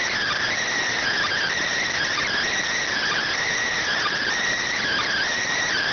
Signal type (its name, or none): Emergency Alert System (SAME) tones